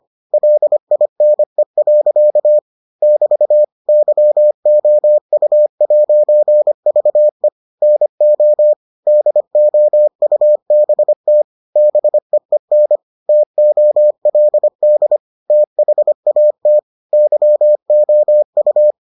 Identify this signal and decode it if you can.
Morse code, heard as a keyed tone, about 25 wpm, 600 Hz. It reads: LINE. <BT> YOU'VE NO DOUBT BEEN TOLD THAT YOU